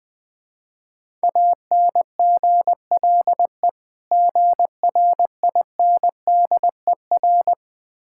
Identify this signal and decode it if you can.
Morse code, heard as a keyed tone, about 20 wpm, 700 Hz. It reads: ANGLE GRINDER
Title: Difficult Words with Hints - 20wpm